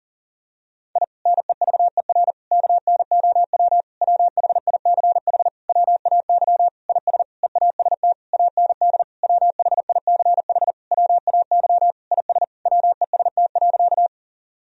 Morse code, heard as a keyed tone, about 40 wpm, 700 Hz. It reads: I NEVER KNOW WHICH WAY IS EAST AND WHICH WAY IS WEST.
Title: Sentences from Top 1000 Words - Review Part 3 - 40wpm